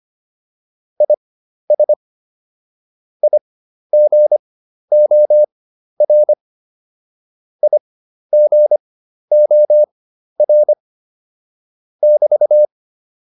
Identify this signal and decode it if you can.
Morse code, heard as a keyed tone, about 25 wpm, spaced as for 12 wpm, 600 Hz. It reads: IS IGOR IGOR <BT>